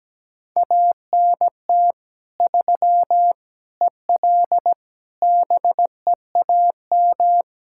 Morse code, heard as a keyed tone, about 17 wpm, 700 Hz. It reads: ANT 3 EL BEAM
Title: Practice QSO - Exchange 2 - Info - 17wpm